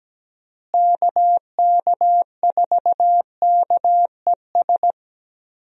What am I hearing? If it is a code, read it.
Morse code, heard as a keyed tone, about 17 wpm, 700 Hz. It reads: KK4KES
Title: USA Call Signs - 6 Characters - Repeated 2x - 17wpm